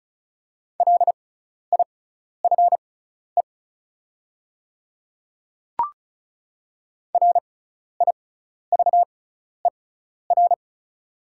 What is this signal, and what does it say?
Morse code, heard as a keyed tone, about 36 wpm, spaced as for 12 wpm, 700 Hz. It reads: LIFE RIVER
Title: 6x Character Spacing - Top 500 Words - 35wpm